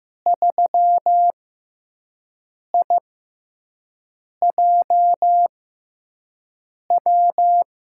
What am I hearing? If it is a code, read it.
Morse code, heard as a keyed tone, about 15 wpm, spaced as for 5 wpm, 700 Hz. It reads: 3IJW
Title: USA Call Signs - 5 Characters - 6x Character Spacing - 15wpm